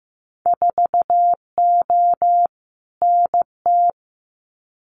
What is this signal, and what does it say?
Morse code, heard as a keyed tone, about 15 wpm, 700 Hz. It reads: 4O NT
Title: Winter Field Day Exchanges Encore 15wpm